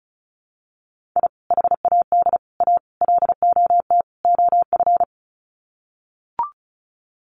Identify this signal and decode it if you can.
Morse code, heard as a keyed tone, about 34 wpm, 700 Hz. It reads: I HAD A LOT OF
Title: Sets of 5 Words Encore 35wpm